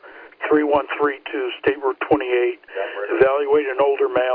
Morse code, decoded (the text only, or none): none